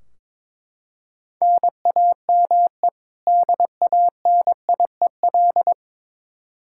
Morse code, heard as a keyed tone, about 22 wpm, 700 Hz. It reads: NAME DANIEL
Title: Practice QSO - Exchange 1 - Beginning - 22wpm